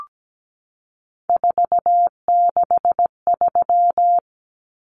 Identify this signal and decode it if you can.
Morse code, heard as a keyed tone, about 17 wpm, 700 Hz. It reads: 463
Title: Common Cut Numbers - 3-Digit 17wpm